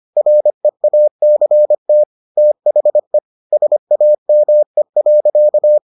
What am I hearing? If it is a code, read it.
Morse code, heard as a keyed tone, about 25 wpm, 600 Hz. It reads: REACT THE SAME.